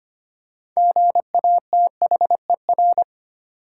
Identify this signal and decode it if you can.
Morse code, heard as a keyed tone, about 25 wpm, 700 Hz. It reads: GATHER
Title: Six Letter Words - 25wpm